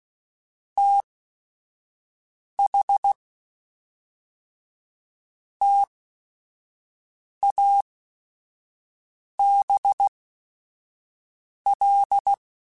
Morse code, heard as a keyed tone, about 16 wpm, spaced as for 5 wpm, 800 Hz. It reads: TH TABL
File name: Technical Subjects / Technical Subjects-mp3s / AND Gate @5